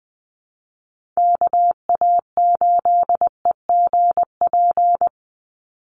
Morse code, heard as a keyed tone, about 20 wpm, 700 Hz. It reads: KA8EGP
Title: USA Call Signs 20wpm